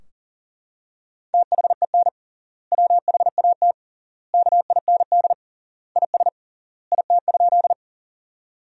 Morse code, heard as a keyed tone, about 40 wpm, 700 Hz. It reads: THEN WHAT KIND IS IT?
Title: 3x Word-Spacing - Sentences from Top 200 Words - 40wpm